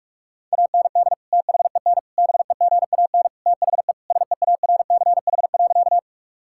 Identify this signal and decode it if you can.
Morse code, heard as a keyed tone, about 46 wpm, 700 Hz. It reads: AND THEN BEGAN THE SEARCH.
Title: Sentences from Top 1000 Words - Review Part 5 - 45wpm